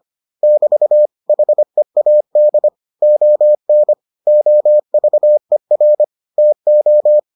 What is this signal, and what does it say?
Morse code, heard as a keyed tone, about 25 wpm, 600 Hz. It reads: <BT> HEAD ON OVER TO